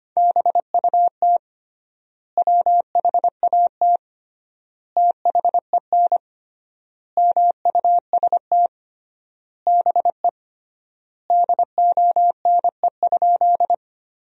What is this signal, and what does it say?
Morse code, heard as a keyed tone, about 25 wpm, 700 Hz. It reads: BUT WHAT THEN MUST BE DONE?
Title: 3x Word-Spacing - Sentences from Top 500 Words - 25wpm